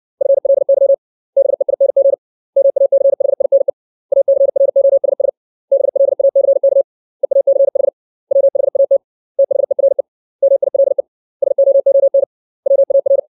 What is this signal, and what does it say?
Morse code, heard as a keyed tone, about 60 wpm, 550 Hz. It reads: FLY BEING MACHINE ACROSS BLACK EACH WHAT THERE MILE SOON WAR
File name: 632 words / 632 Words 60WPM